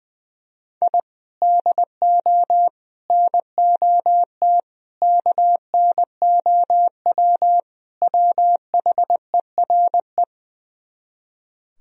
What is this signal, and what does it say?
Morse code, heard as a keyed tone, about 21 wpm, 700 Hz. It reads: I DO NOT KNOW WHERE
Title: Speed-Racing - Sets of 5 Words 20wpm